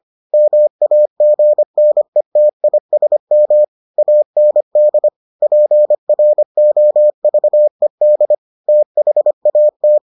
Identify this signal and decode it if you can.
Morse code, heard as a keyed tone, about 25 wpm, 600 Hz. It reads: MAGNETISM AND PROVED THAT